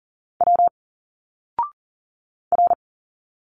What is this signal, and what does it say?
Morse code, heard as a keyed tone, about 40 wpm, 700 Hz. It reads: W R
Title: Single Letters 40wpm